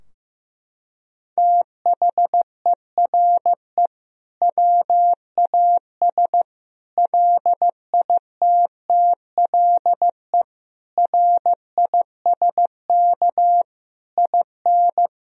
Morse code, heard as a keyed tone, about 15 wpm, 700 Hz. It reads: THERE WAS LITTLE RISK IN
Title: Sentences from Top 1000 Words - Review Part 4 - 15wpm